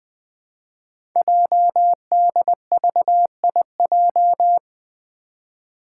Morse code, heard as a keyed tone, about 20 wpm, 700 Hz. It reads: JDVIJ
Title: Code Groups - 5 Letter:Numbers - 20wpm